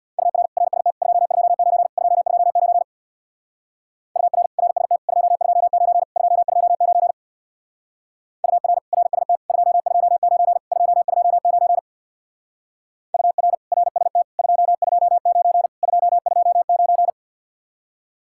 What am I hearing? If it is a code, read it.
Morse code, heard as a keyed tone, about 60 wpm, 700 Hz. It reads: UR RST 229 229 UR RST 229 229 UR RST 229 229 UR RST 229 229
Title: Speed-Racing Common QSO Phrases - 50wpm